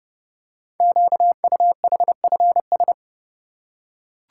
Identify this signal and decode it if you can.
Morse code, heard as a keyed tone, about 30 wpm, 700 Hz. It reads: QUHFS
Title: Code Groups - 5 Letters - 30wpm